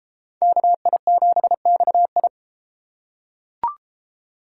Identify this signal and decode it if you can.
Morse code, heard as a keyed tone, about 33 wpm, 700 Hz. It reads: KI7XI